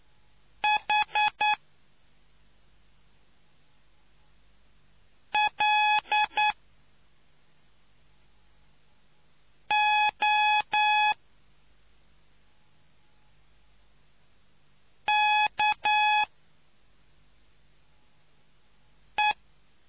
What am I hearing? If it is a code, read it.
Morse code, heard as a keyed tone, about 9 wpm, 840 Hz. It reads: H L O K E